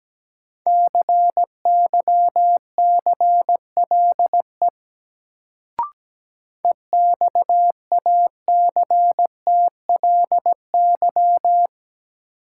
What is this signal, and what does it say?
Morse code, heard as a keyed tone, about 17 wpm, 700 Hz. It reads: CYCLE EXACTLY
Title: Sentences from Top 900 Words - New Word Drill - 17wpm